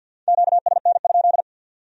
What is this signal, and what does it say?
Morse code, heard as a keyed tone, about 61 wpm, 700 Hz. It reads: QSN?
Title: Speed-Racing Q-Codes - 50wpm